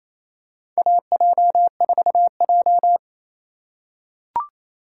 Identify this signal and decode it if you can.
Morse code, heard as a keyed tone, about 28 wpm, 700 Hz. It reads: AJ4J
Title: USA Call Signs - 4 Characters - 2x Character Spacing - 28wpm